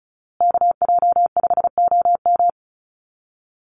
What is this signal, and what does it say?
Morse code, heard as a keyed tone, about 34 wpm, 700 Hz. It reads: KJ5OM
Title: USA Call Signs Repeated 2x 35wpm